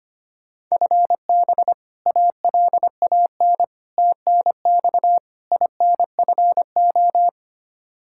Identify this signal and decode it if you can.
Morse code, heard as a keyed tone, about 25 wpm, 700 Hz. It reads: FB ALAN TNX INFO
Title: Practice QSO - Exchange 3 - More Info - 25wpm